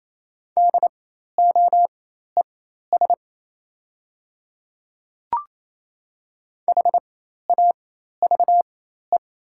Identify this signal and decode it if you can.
Morse code, heard as a keyed tone, about 28 wpm, spaced as for 13 wpm, 700 Hz. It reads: DOES HAVE
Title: 4x Character Spacing - Top 300 Words - 28wpm